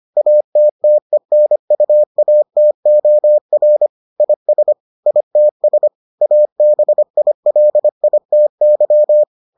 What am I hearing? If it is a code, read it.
Morse code, heard as a keyed tone, about 25 wpm, 600 Hz. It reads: ATTENUATOR IS ITS ABILITY